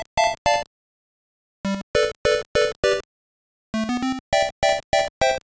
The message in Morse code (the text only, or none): none